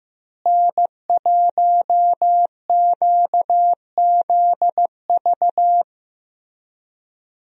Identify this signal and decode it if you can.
Morse code, heard as a keyed tone, about 16 wpm, 700 Hz. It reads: N1QZV
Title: USA Call Signs - 5 Characters - Repeated 2x - 15wpm